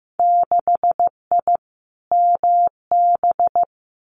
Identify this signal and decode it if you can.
Morse code, heard as a keyed tone, about 15 wpm, 700 Hz. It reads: 6I MB
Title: Winter Field Day Exchanges Encore 15wpm